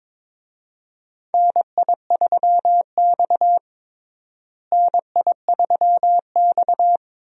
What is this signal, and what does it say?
Morse code, heard as a keyed tone, about 22 wpm, 700 Hz. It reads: NI3X NI3X
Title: USA Call Signs - 4 Characters - Repeated 2x - 22wpm